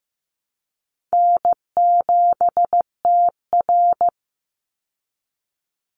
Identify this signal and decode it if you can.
Morse code, heard as a keyed tone, about 15 wpm, 700 Hz. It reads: N7TR
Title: Call Signs Repeated 2x - 15wpm